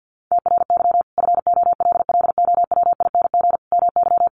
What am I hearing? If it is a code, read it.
Morse code, heard as a keyed tone, about 50 wpm, 700 Hz. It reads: TRY FOLLOWING MY
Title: Sentences from Top 1100 Words 50wpm